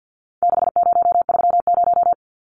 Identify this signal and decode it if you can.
Morse code, heard as a keyed tone, about 48 wpm, 700 Hz. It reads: 6030
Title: Numbers - 4-Digit 50wpm